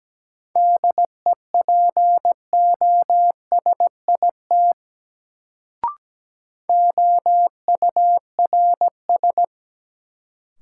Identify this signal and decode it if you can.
Morse code, heard as a keyed tone, about 17 wpm, 700 Hz. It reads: DEPOSIT OURS
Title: Sentences from Top 1300 Words - New Word Drill Alt - 17wpm